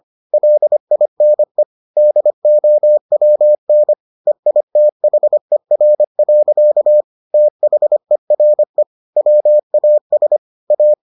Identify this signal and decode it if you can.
Morse code, heard as a keyed tone, about 25 wpm, 600 Hz. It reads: LINE DOWN EITHER. THERE WAS A